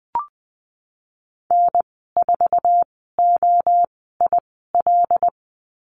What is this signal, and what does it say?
Morse code, heard as a keyed tone, about 20 wpm, spaced as for 15 wpm, 700 Hz. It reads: N4OIL
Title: 167 Lesson 27g - Number 7 - Call Signs 20wpm